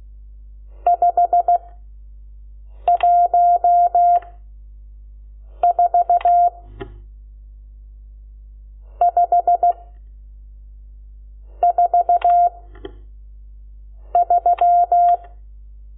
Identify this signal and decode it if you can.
Morse code, heard as a keyed tone, about 16 wpm, 700 Hz. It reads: 5 1 4 5 4 3
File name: Lecciones / CLASE Nº12